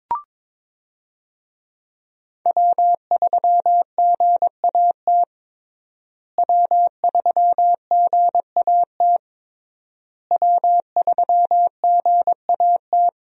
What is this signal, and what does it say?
Morse code, heard as a keyed tone, about 22 wpm, 700 Hz. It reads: W3GAT W3GAT W3GAT
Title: USA Call Signs - 5 Characters - Repeated 3x - 22wpm